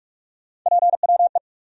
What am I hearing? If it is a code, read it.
Morse code, heard as a keyed tone, about 45 wpm, 700 Hz. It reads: PWE
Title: USA Call Sign Postfixes - 3 Letters 45wpm